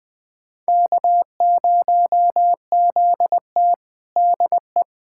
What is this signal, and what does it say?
Morse code, heard as a keyed tone, about 20 wpm, 700 Hz. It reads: K0ZT DE
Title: USA Callsign DE Callsign - 4 Characters - 20wpm